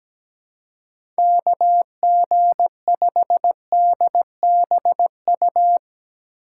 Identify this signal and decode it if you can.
Morse code, heard as a keyed tone, about 17 wpm, 700 Hz. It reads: KG5DBU
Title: USA Call Signs - 4x Character Spacing - 17wpm